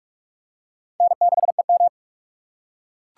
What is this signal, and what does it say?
Morse code, heard as a keyed tone, about 43 wpm, 700 Hz. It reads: N6EM